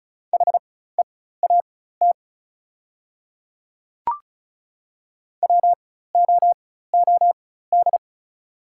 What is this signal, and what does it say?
Morse code, heard as a keyed tone, about 36 wpm, spaced as for 16 wpm, 700 Hz. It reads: HEAT WOOD
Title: 4x Character Spacing - Top 500 Words - 35wpm